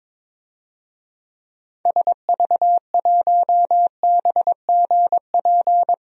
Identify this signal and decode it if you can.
Morse code, heard as a keyed tone, about 22 wpm, 700 Hz. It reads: SV1BGP
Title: DXCC-100 Call Signs - 6 Characters - 22wpm